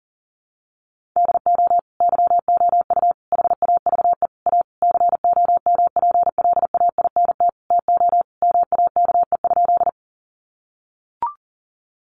Morse code, heard as a keyed tone, about 40 wpm, 700 Hz. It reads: DO YOU HAVE A COMPLAINT TO MAKE?
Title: Sentences from Top 1400 Words 40wpm